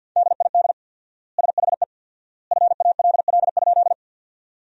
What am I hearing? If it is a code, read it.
Morse code, heard as a keyed tone, about 53 wpm, 700 Hz. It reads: DID SHE FALL?
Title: 4x Word-Spacing - Sentences from Top 500 Words - 50wpm